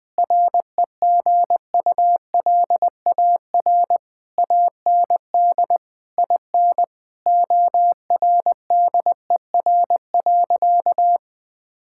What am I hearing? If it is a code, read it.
Morse code, heard as a keyed tone, about 21 wpm, 700 Hz. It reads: REGULAR AND IN ORDER.
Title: Sentences from Top 1000 Words - Review Part 6 - 20wpm